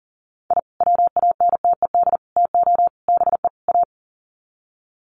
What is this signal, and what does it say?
Morse code, heard as a keyed tone, about 40 wpm, 700 Hz. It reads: I WANTED TO BE A
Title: Sets of 5 Words Encore2 40wpm